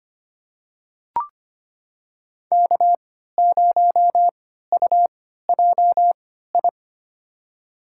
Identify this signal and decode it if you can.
Morse code, heard as a keyed tone, about 25 wpm, spaced as for 14 wpm, 700 Hz. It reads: K0UJI
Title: USA Call Signs - 5 Characters - 3x Character Spacing - 25wpm